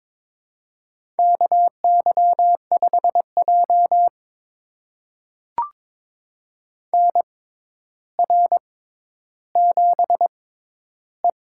Morse code, heard as a keyed tone, about 22 wpm, spaced as for 11 wpm, 700 Hz. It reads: KY5J NR7E
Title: USA Call Signs - 4 Characters - 6x Character Spacing - 22wpm